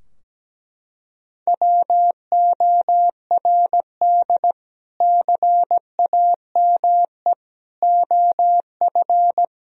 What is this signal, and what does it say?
Morse code, heard as a keyed tone, about 17 wpm, 700 Hz. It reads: WORD CAME OF
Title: Sentences from Top 1000 Words - Review Part 4 - 17wpm